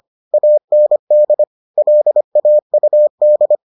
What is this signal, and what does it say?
Morse code, heard as a keyed tone, about 25 wpm, 600 Hz. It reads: AND LAUD